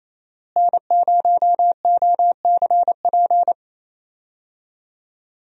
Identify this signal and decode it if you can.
Morse code, heard as a keyed tone, about 28 wpm, 700 Hz. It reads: N0OCP
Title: USA Call Signs - 5 Characters - Repeated 3x - 28wpm